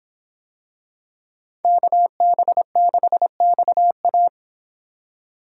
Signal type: Morse code

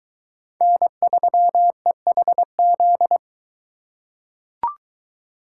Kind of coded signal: Morse code

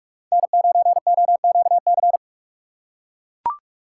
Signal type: Morse code

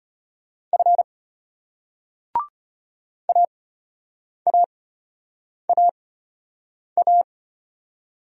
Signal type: Morse code

